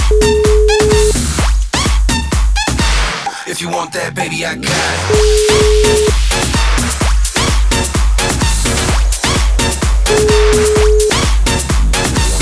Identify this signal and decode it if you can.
Ringback tone (Europe, 425 Hz), with noise.